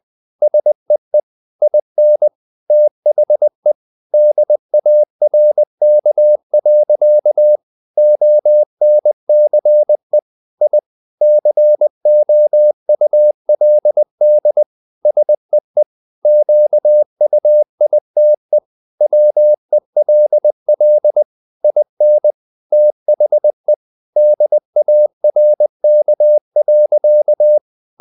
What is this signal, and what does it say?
Morse code, heard as a keyed tone, about 20 wpm, 600 Hz. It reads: SEE IN THE DARK. ONCE I COULD SEE QUITE WELL IN THE DARK.
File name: the old man and the sea / the old man and the sea part 1 20wpm